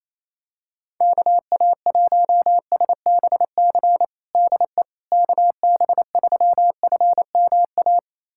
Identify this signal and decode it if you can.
Morse code, heard as a keyed tone, about 28 wpm, 700 Hz. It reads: KA1SBC DE KB3FMA
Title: USA Callsign DE Callsign - 6 Characters - 28wpm